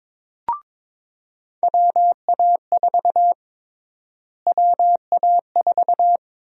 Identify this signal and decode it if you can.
Morse code, heard as a keyed tone, about 22 wpm, 700 Hz. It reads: WA4 WA4
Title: USA Call Sign Prefixes - Repeated 2x - 22wpm